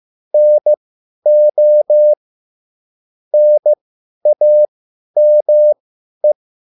Morse code, heard as a keyed tone, about 15 wpm, spaced as for 10 wpm, 600 Hz. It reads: NO NAME
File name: cwbook